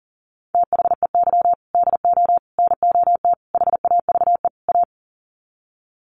Morse code, heard as a keyed tone, about 37 wpm, 700 Hz. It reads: THEY DO NOT HAVE A
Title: Sets of 5 Words 40wpm